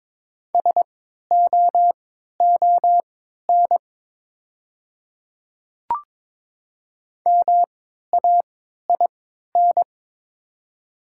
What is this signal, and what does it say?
Morse code, heard as a keyed tone, about 23 wpm, spaced as for 13 wpm, 700 Hz. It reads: SOON MAIN
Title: 3x Character Spacing - Top 500 Words - 22wpm